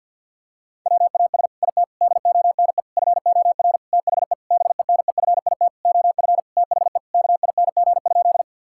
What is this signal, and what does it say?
Morse code, heard as a keyed tone, about 50 wpm, 700 Hz. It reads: WAS IT DONE FOR THE BENEFIT OF THE KING?
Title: Sentences from Top 1000 Words - Review Part 6 - 50wpm